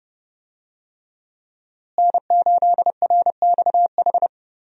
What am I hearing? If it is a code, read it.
Morse code, heard as a keyed tone, about 30 wpm, 700 Hz. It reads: N8RXH
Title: USA Call Signs - 5 Characters - Repeated 3x - 30wpm